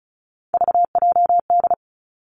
Morse code, heard as a keyed tone, about 36 wpm, 700 Hz. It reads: VJD
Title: Three Challenging Letter-Number Combinations 35wpm